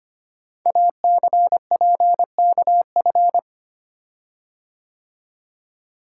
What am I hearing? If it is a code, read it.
Morse code, heard as a keyed tone, about 25 wpm, 690 Hz. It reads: ACPKF
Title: Code Groups - 5 Letter:Numbers - 25wpm